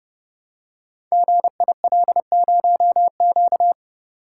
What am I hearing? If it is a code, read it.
Morse code, heard as a keyed tone, about 30 wpm, 700 Hz. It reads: GIL0Q
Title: Code Groups - 5 Letter:Numbers - 30wpm